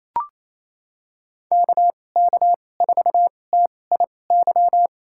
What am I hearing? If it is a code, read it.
Morse code, heard as a keyed tone, about 28 wpm, spaced as for 20 wpm, 700 Hz. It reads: KK4TIY
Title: USA Call Signs - 2x Character Spacing - 28wpm